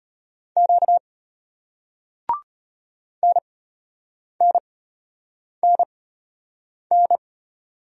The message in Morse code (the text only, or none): Q NNNN